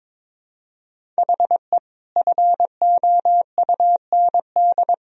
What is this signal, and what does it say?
Morse code, heard as a keyed tone, about 22 wpm, 700 Hz. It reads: HE FOUND
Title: Sentences from Top 1000 Words - Review Part 1 - 22wpm